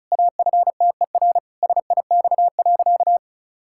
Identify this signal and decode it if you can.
Morse code, heard as a keyed tone, about 36 wpm, 700 Hz. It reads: AFTER SIX.
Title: Pangram Challenge v2 - 35wpm